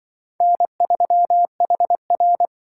Morse code, heard as a keyed tone, about 24 wpm, 700 Hz. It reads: N3HR
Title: Speed Racing - USA 1x2 Call Signs 20wpm